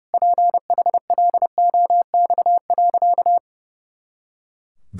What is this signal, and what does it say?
Morse code, heard as a keyed tone, about 30 wpm, 700 Hz. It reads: PHLOX.
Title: Pangram Challenge v2 - 30wpm